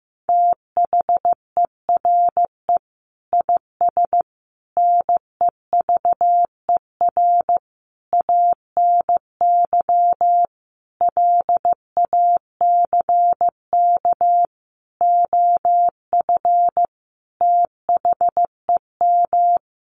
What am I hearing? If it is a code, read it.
Morse code, heard as a keyed tone, about 15 wpm, 700 Hz. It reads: THERE IS NEVER ANY LACK OF THEM
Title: Sentences from Top 1100 Words 15wpm